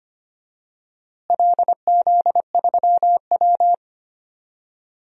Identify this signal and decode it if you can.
Morse code, heard as a keyed tone, about 25 wpm, 700 Hz. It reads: LZ3W
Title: DXCC-100 Call Signs - 4 Characters - Repeated 3x - 25wpm